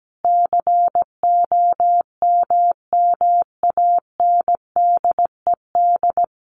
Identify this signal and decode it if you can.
Morse code, heard as a keyed tone, about 17 wpm, 700 Hz. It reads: COMMANDED
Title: A Sound Of Thunder 17wpm